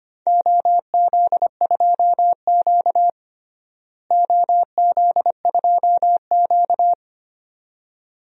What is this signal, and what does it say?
Morse code, heard as a keyed tone, about 25 wpm, 700 Hz. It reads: OZ2Q OZ2Q
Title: DXCC-100 Call Signs - 4 Characters - Repeated 2x - 25wpm